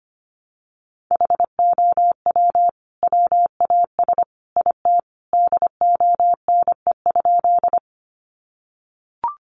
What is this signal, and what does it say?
Morse code, heard as a keyed tone, about 25 wpm, 700 Hz. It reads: HOW WAS IT DONE?
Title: Sentences from Top 500 Words 25wpm